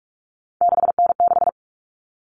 Morse code, heard as a keyed tone, about 44 wpm, 700 Hz. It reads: 6N6
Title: All Cut Numbers - 3-Digit 45wpm